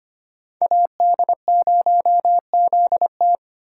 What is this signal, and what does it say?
Morse code, heard as a keyed tone, about 25 wpm, 700 Hz. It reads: AD0ZT
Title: USA Call Signs - 5 Characters - 4x Character Spacing - 25wpm